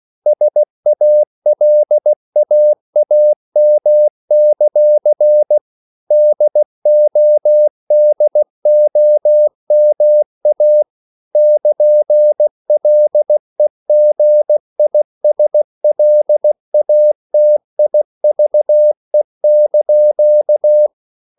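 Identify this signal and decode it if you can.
Morse code, heard as a keyed tone, about 16 wpm, 600 Hz. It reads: SALAAM; DODOMA <KN>LEGISLATIVE)
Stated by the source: CW practice of national capitals at 16 words per minute, from ON6ZQ.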